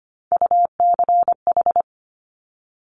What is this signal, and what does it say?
Morse code, heard as a keyed tone, about 25 wpm, 700 Hz. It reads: UCH
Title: Top Three Letter Combinations in Words 25wpm